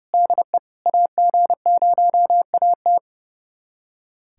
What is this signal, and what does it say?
Morse code, heard as a keyed tone, about 32 wpm, 700 Hz. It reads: DE AG0AT
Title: USA Callsign DE Callsign - 5 Characters - 30wpm